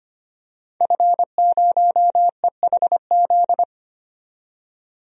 Morse code, heard as a keyed tone, about 25 wpm, 700 Hz. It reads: F0EHZ
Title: DXCC-100 Call Signs - 5 Characters - 25wpm